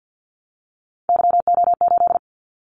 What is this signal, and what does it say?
Morse code, heard as a keyed tone, about 50 wpm, 700 Hz. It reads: YO8
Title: Callsign Prefixes for Top 25 DXCC - 50wpm